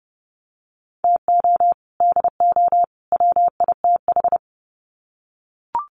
Morse code, heard as a keyed tone, about 30 wpm, 700 Hz. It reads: TO DO WITH
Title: Sets of 3 Words 30wpm